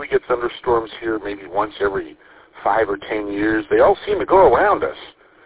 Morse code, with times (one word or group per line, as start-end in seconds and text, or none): none